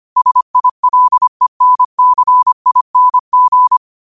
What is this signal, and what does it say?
Morse code, heard as a keyed tone, about 25 wpm, 1000 Hz. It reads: SILENCING
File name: qotdfiles / 25wpm